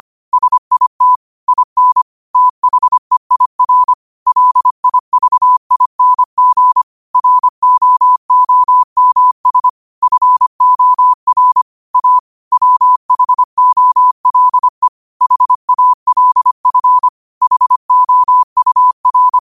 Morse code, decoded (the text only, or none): SIT IN THEIR LIVING ROOMS FOR A WHOLE HALF HOUR